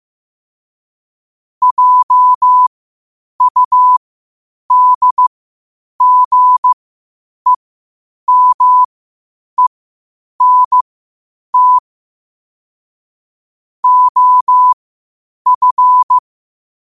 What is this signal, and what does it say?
Morse code, heard as a keyed tone, about 15 wpm, spaced as for 8 wpm, 1000 Hz. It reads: JUDGEMENT OF
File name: qotdfiles / 8wpm